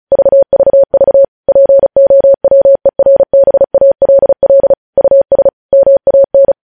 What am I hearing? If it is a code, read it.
Morse code, heard as a keyed tone, about 36 wpm, 550 Hz. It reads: VVV POWERBALL US MAN
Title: News Headlines at 35